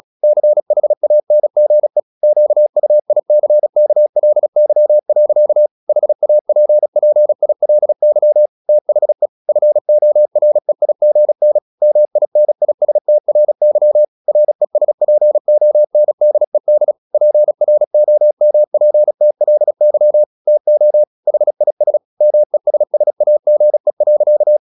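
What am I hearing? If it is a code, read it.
Morse code, heard as a keyed tone, about 36 wpm, 600 Hz. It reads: CHANGE QUICKLY. HAPPILY THE FOREIGN MINISTRY RESPONDED PROMPTLY TO HIS MESSAGE.
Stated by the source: Morse practice at 35 wpm from CW med Gnister, sending Ken Follett's novel Never in English.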